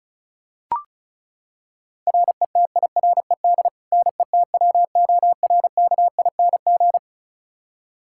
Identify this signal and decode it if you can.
Morse code, heard as a keyed tone, about 34 wpm, 700 Hz. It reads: RETIRED NETWORKING
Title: K1USN SST QSO Practice 35wpm